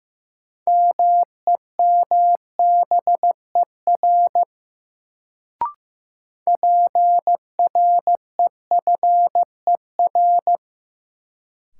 Morse code, heard as a keyed tone, about 15 wpm, 700 Hz. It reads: MEMBER PREFER
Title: Sentences from Top 1100 Words - New Word Drill Alt - 15wpm